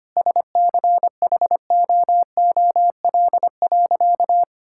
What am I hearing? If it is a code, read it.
Morse code, heard as a keyed tone, about 25 wpm, 700 Hz. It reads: SCHOOL.
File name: practice / 25wpm / Sentences with 3-Word Phrases - 25wpm